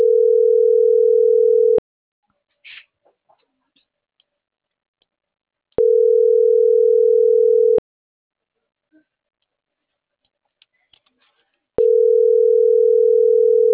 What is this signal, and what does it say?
Ringback tone (North America), 440 + 480 Hz.